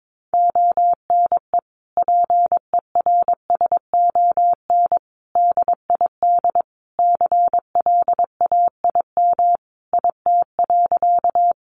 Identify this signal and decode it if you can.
Morse code, heard as a keyed tone, about 22 wpm, 700 Hz. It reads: ONE PERSON DID CLAIM IT.
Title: Sentences from Top 1500 Words 22wpm